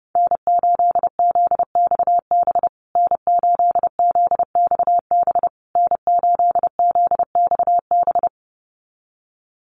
Morse code, heard as a keyed tone, about 30 wpm, 700 Hz. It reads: N8ZXB N8ZXB N8ZXB
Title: USA Call Signs Repeated 3x 30wpm